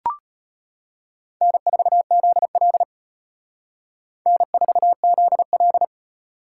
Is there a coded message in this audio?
Yes: Morse code from 1.4 to 5.8 s, heard as a keyed tone, about 32 wpm, 700 Hz. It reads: N4ZL N4ZL